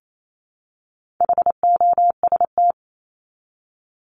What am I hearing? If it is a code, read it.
Morse code, heard as a keyed tone, about 28 wpm, 700 Hz. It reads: HOST
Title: Top 1501-1750 Words 28wpm